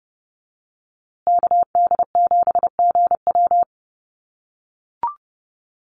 Morse code, heard as a keyed tone, about 28 wpm, 700 Hz. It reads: KD7GW
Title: USA Call Signs Repeated 2x 30wpm